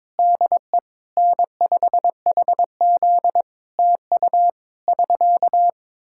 Morse code, heard as a keyed tone, about 22 wpm, 700 Hz. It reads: DE N5HZ TU <SK>
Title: Practice QSO - Exchange 4 - Ending - 22wpm